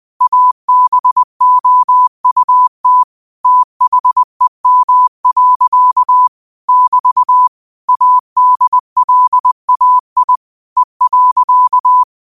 Morse code, heard as a keyed tone, about 20 wpm, 1000 Hz. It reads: ABOUT THEM. <BT> ADLAI E.